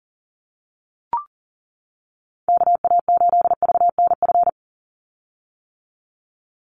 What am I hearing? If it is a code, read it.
Morse code, heard as a keyed tone, about 40 wpm, 700 Hz. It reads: KA8VNF
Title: USA Call Signs 40wpm